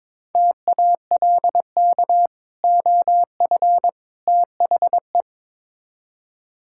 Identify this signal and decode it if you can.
Morse code, heard as a keyed tone, about 22 wpm, 700 Hz. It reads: TALK OF THE
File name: practice / 22wpm / Build A Phrase - 5-Word Phrases - 22wpm